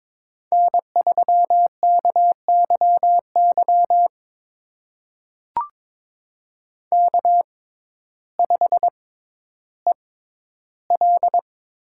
Morse code, heard as a keyed tone, about 22 wpm, spaced as for 12 wpm, 700 Hz. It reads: N3KYY K5EL
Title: USA Call Signs - 5 Characters - 6x Character Spacing - 22wpm